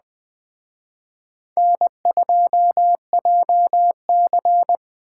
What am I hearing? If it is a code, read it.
Morse code, heard as a keyed tone, about 19 wpm, 700 Hz. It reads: N2JC